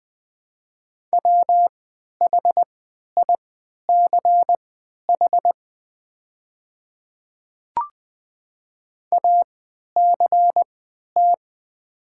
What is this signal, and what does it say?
Morse code, heard as a keyed tone, about 20 wpm, spaced as for 11 wpm, 700 Hz. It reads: WHICH ACT
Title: 3x Character Spacing - Top 200 Words - 20wpm